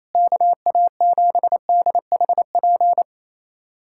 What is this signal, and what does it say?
Morse code, heard as a keyed tone, about 28 wpm, 700 Hz. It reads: KA7DHP
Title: USA Callsign DE Callsign 28wpm